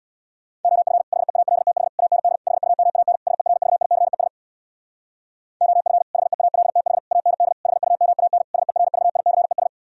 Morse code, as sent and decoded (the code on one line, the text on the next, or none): --- ..-. / .... .- .-.. - .... / .- -. -.. / .... ..- -- .- -. / ... .-. ...- .. -.-. ... / --- ..-. / .... .- .-.. - .... / .- -. -.. / .... ..- -- .- -. / ... .-. ...- .. -.-. ...
OF HEALTH AND HUMAN SERVICES OF HEALTH AND HUMAN SERVICES